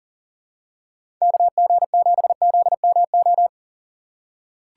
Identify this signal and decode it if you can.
Morse code, heard as a keyed tone, about 40 wpm, 700 Hz. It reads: KG7ZMO
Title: USA Call Signs - 2x Character Spacing - 40wpm